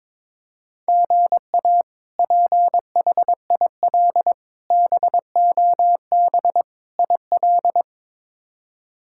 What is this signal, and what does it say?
Morse code, heard as a keyed tone, about 22 wpm, 700 Hz. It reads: GA PHIL BOB IL
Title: K1USN SST QSO Practice - 22wpm